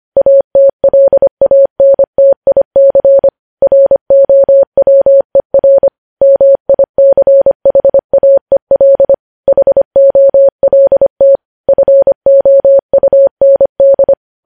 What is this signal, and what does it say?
Morse code, heard as a keyed tone, about 25 wpm, 550 Hz. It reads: ATLANTIC ROWER MICHAEL HOLT FOUND
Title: News Headlines at 25